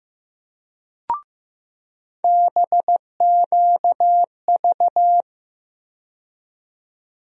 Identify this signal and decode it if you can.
Morse code, heard as a keyed tone, about 15 wpm, 700 Hz. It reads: BQV
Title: USA Call Sign Postfixes - 3 Letters 15wpm